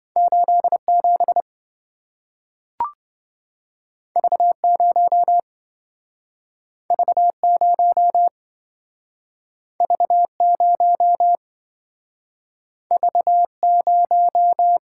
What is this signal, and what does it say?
Morse code, heard as a keyed tone, about 27 wpm, 700 Hz. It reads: 87 V0 V0 V0 V0